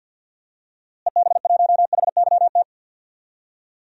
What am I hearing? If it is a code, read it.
Morse code, heard as a keyed tone, about 52 wpm, 700 Hz. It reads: EB1HYT
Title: DXCC-100 Call Signs - 6 Characters - Repeated 3x - 50wpm